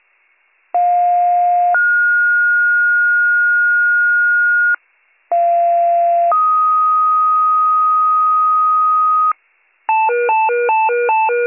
Two-tone page, heard as a one-second tone, then a three-second tone, about 710 then 1430 Hz, about 690 then 1250 Hz, with noise. It is heard from 0.7 to 9.3 s.